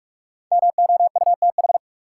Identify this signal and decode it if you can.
Morse code, heard as a keyed tone, about 46 wpm, 700 Hz. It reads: MOUTH
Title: Sentences from Top 1500 Words - New Word Drill Alt - 45wpm